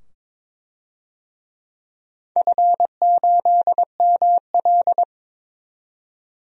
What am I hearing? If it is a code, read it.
Morse code, heard as a keyed tone, about 21 wpm, 700 Hz. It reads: F8ML
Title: DXCC-100 Call Signs - 4 Characters - 22wpm